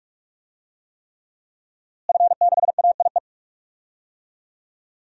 Morse code, heard as a keyed tone, about 47 wpm, 700 Hz. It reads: F6AIE
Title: DXCC-100 Call Signs - 5 Characters - Repeated 3x - 45wpm